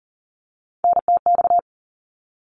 Morse code, heard as a keyed tone, about 40 wpm, 700 Hz. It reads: NTX
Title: ARRL Sections 40wpm